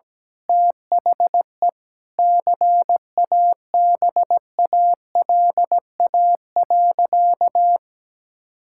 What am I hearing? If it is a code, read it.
Morse code, heard as a keyed tone, about 17 wpm, 700 Hz. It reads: THE CABALA.
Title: Pangram Challenge v2 - 17wpm